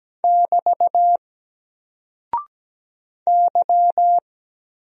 Morse code, heard as a keyed tone, about 17 wpm, 700 Hz. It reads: <BT> Y
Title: Single Character (Letter:Number:Punctuation:Prosign) 17wpm